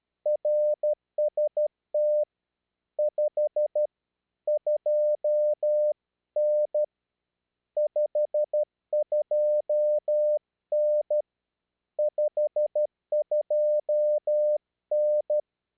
Morse code, heard as a keyed tone, about 12 wpm, 600 Hz. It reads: RST 52N 52N 52N
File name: eme / rw3bp